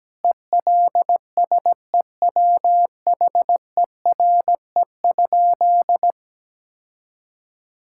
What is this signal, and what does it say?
Morse code, heard as a keyed tone, about 17 wpm, 700 Hz. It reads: ELSEWHERE?
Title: Sentences from Top 1600 Words 17wpm